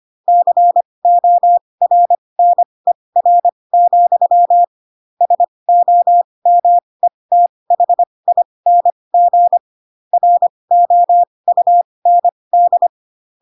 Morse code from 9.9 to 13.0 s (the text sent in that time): ROUND